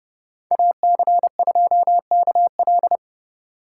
Morse code, heard as a keyed tone, about 29 wpm, 700 Hz. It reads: AC2KL
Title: USA Call Signs - 2x Character Spacing - 30wpm